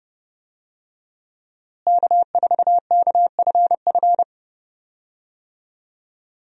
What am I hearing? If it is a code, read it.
Morse code, heard as a keyed tone, about 32 wpm, 700 Hz. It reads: K4KFF
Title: USA Call Signs - 5 Characters - Repeated 3x - 30wpm